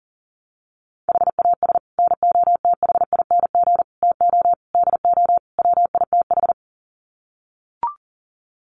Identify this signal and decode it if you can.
Morse code, heard as a keyed tone, about 40 wpm, 700 Hz. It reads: HAS NOTHING TO DO WITH
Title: Sets of 5 Words 40wpm